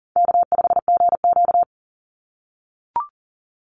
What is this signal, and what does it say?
Morse code, heard as a keyed tone, about 40 wpm, 700 Hz. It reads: K5GQ
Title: Call Signs - 40wpm